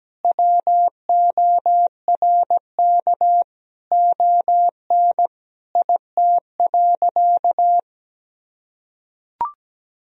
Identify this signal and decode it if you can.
Morse code, heard as a keyed tone, about 17 wpm, 700 Hz. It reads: WORK ON IT.
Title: Sentences from Top 1000 Words - Review Part 3 - 17wpm